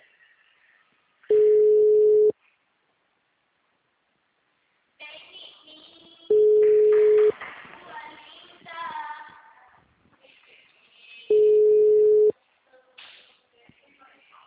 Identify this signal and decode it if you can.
Ringback tone (Europe, 425 Hz), with speech.